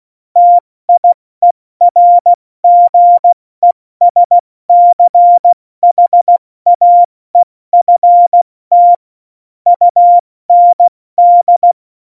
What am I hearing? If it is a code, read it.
Morse code, heard as a keyed tone, about 16 wpm, 700 Hz. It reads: TIERGESCHAEFT UND
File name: Morsekurs / Witze / Witz 20. November 2018, 70 F80